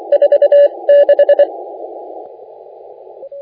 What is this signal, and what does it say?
Morse code, heard as a keyed tone, about 23 wpm, spaced as for 20 wpm, 570 Hz. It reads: 46
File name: la4c / sac-cw-2018 / CONTEST AUDIO 1600Z